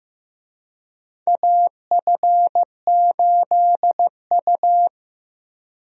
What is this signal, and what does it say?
Morse code, heard as a keyed tone, about 15 wpm, 700 Hz. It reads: AF8U
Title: USA Call Signs - 4 Characters - Repeated 2x - 15wpm